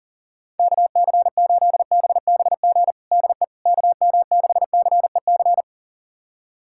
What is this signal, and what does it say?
Morse code, heard as a keyed tone, about 37 wpm, 700 Hz. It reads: KC8BBG DE KM6CEC
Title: USA Callsign DE Callsign 40wpm